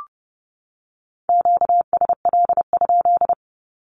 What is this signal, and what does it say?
Morse code, heard as a keyed tone, about 30 wpm, 700 Hz. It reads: QSL?
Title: Q-Codes 30wpm